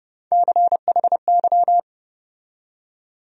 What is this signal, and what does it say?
Morse code, heard as a keyed tone, about 30 wpm, 700 Hz. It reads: CHY
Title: USA Call Sign Postfixes - 3 Letters 30wpm